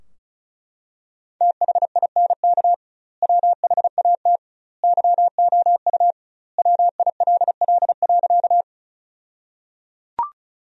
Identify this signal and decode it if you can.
Morse code, heard as a keyed tone, about 36 wpm, 700 Hz. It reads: THINK WHAT YOU WILL.
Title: 2x Word-Spacing - Sentences from Top 200 Words - 35wpm